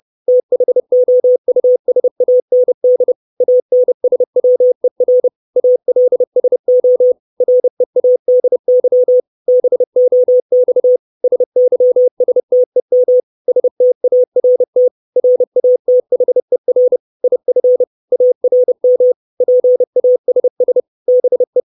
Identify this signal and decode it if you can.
Morse code, heard as a keyed tone, about 30 wpm, 500 Hz. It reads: THOUSAND ANSWER ALSO READY BOX SYSTEM START RATHER IF ARM PASS BE
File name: 632 Words 30WPM